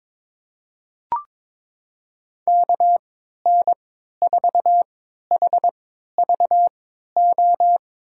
Morse code, heard as a keyed tone, about 22 wpm, spaced as for 12 wpm, 700 Hz. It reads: KN4HVO
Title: USA Call Signs - 6 Characters - 3x Character Spacing - 22wpm